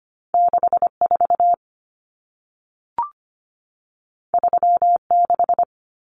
Morse code, heard as a keyed tone, about 25 wpm, 700 Hz. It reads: 64 36
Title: Common Cut Numbers - 2-Digit 25wpm